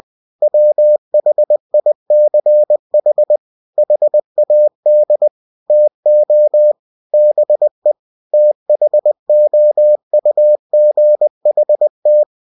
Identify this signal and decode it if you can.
Morse code, heard as a keyed tone, about 20 wpm, 600 Hz. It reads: WHICH HAD TO BE THOUGHT
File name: cw-books / winnie-the-pooh / winnie-20wpm-page-0035